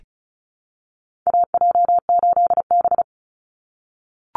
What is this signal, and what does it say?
Morse code, heard as a keyed tone, about 34 wpm, 700 Hz. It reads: AJ8B
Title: Call Signs - 35wpm